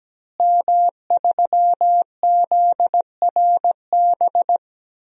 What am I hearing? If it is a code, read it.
Morse code, heard as a keyed tone, about 17 wpm, 700 Hz. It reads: M3ZRB
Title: DXCC-100 Call Signs - 5 Characters - Repeated 3x - 17wpm